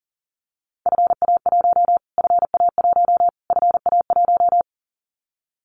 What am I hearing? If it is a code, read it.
Morse code, heard as a keyed tone, about 40 wpm, 700 Hz. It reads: FA1 FA1 FA1
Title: Callsign Prefixes for Top 25 DXCC - Repeated 3x - 40wpm